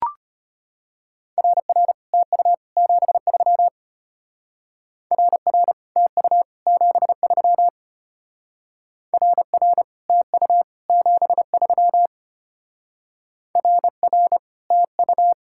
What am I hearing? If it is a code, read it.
Morse code, heard as a keyed tone, about 32 wpm, 700 Hz. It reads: RR TU 73 RR TU 73 RR TU 73 RR TU